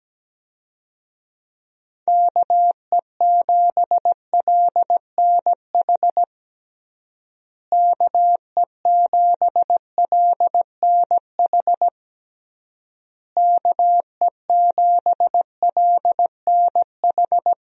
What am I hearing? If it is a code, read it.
Morse code, heard as a keyed tone, about 17 wpm, 700 Hz. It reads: KE7LNH KE7LNH KE7LNH
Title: USA Call Signs - 6 Characters - Repeated 3x - 17wpm